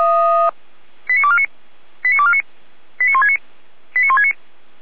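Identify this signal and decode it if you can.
Five-tone selective call (ZVEI), heard as quick runs of five tones, tones of about 70 ms. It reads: 89249 89249 89169 89169
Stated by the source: ZVEI five-tone selective call received on SDR by feuerwehr-orlamuende fire brigade.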